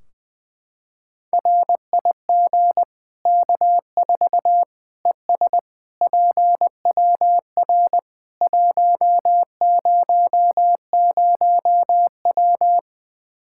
Morse code, heard as a keyed tone, about 20 wpm, 700 Hz. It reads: RIG K4 ES PWR 100W
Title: Practice QSO - Exchange 2 - Info - 20wpm